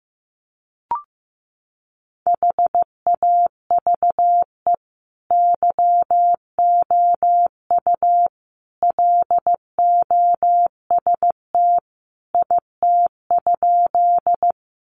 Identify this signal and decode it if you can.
Morse code, heard as a keyed tone, about 15 wpm, 700 Hz. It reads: HAVE YOU LOST IT?
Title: Sentences from Top 1400 Words 15wpm